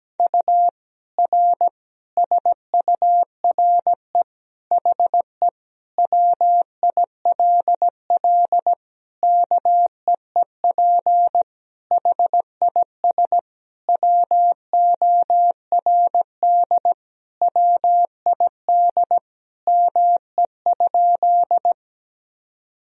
Morse code, heard as a keyed tone, about 17 wpm, 700 Hz. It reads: U R SURE HE WILL KEEP HIS WORD WID ME?
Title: Sentences from Top 300 Words and Abbreviations 17wpm